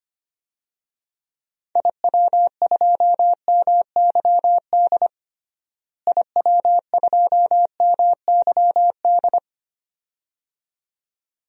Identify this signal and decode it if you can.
Morse code, heard as a keyed tone, about 25 wpm, 700 Hz. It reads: IW2MYD IW2MYD
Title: DXCC-100 Call Signs - 6 Characters - Repeated 2x - 25wpm